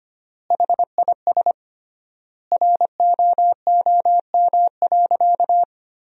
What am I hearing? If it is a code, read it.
Morse code, heard as a keyed tone, about 25 wpm, 700 Hz. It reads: HIS ROOM.
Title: 3x Word-Spacing - Sentences from Top 300 Words - 25wpm